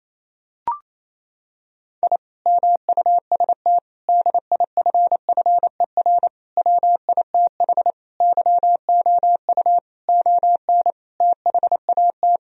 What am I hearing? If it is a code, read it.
Morse code, heard as a keyed tone, about 28 wpm, 700 Hz. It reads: I MUST DIFFER WITH YOU ON THAT
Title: Sentences from Top 1000 Words - Review Part 1 - 28wpm